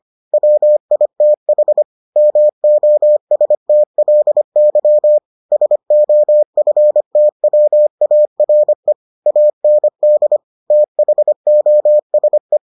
Morse code, heard as a keyed tone, about 25 wpm, 600 Hz. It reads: WITH MOSTLY SOFTWARE AND THOSE